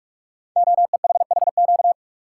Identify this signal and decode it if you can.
Morse code, heard as a keyed tone, about 44 wpm, 700 Hz. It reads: OEHHQ
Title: Code Groups - 5 Letters - 45wpm